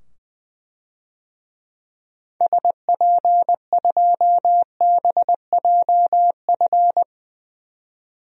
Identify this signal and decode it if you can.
Morse code, heard as a keyed tone, about 20 wpm, 700 Hz. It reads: SP2BJF
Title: DXCC-100 Call Signs - 6 Characters - 20wpm